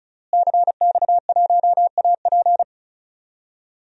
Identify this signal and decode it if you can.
Morse code, heard as a keyed tone, about 36 wpm, 700 Hz. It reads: CX1AP
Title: DXCC-100 Call Signs - 5 Characters - Repeated 2x - 35wpm